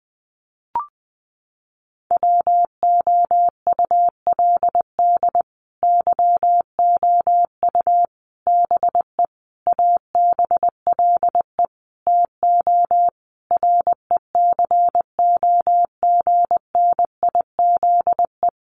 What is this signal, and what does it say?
Morse code, heard as a keyed tone, about 20 wpm, 700 Hz. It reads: WOULD YOU BE ABLE TO RECOGNIZE
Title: Sentences from Top 1400 Words 20wpm